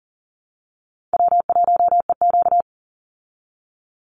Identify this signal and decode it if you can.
Morse code, heard as a keyed tone, about 40 wpm, 700 Hz. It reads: W1EQ
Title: Call Signs Repeated 3x - 40wpm